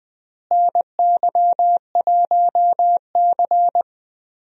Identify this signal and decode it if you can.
Morse code, heard as a keyed tone, about 20 wpm, 700 Hz. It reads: NY1C
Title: USA Call Signs - 4 Characters - Repeated 2x - 20wpm